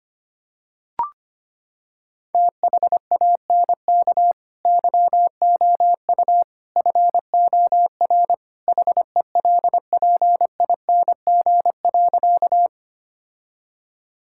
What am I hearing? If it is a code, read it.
Morse code, heard as a keyed tone, about 25 wpm, 700 Hz. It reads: THANK YOU FOR HELPING.
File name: practice / 25wpm / Sentences with 3-Word Phrases - 25wpm